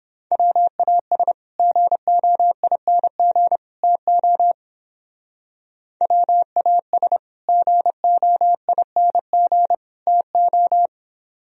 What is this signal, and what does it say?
Morse code, heard as a keyed tone, about 29 wpm, 700 Hz. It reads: WAS GOING TO WAS GOING TO